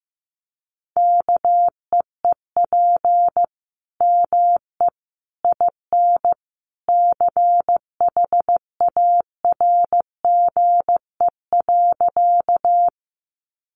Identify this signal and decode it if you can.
Morse code, heard as a keyed tone, about 15 wpm, 700 Hz. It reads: KEEP ME IN CHARGE.
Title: Sentences from Top 900 Words 15wpm